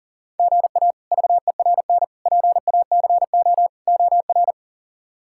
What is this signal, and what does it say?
Morse code, heard as a keyed tone, about 37 wpm, 700 Hz. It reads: GA VERN PACO OR
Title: K1USN SST QSO Practice - 40wpm